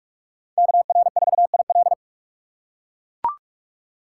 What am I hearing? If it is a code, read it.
Morse code, heard as a keyed tone, about 42 wpm, 700 Hz. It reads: KR4IL
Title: USA Call Signs - 5 Characters - 2x Character Spacing - 45wpm